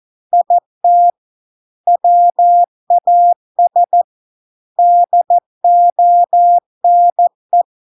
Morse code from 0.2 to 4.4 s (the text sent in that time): IT WAS